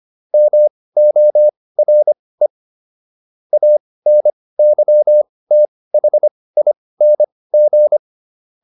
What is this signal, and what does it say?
Morse code, heard as a keyed tone, about 25 wpm, spaced as for 18 wpm, 600 Hz. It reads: MORE ANYTHING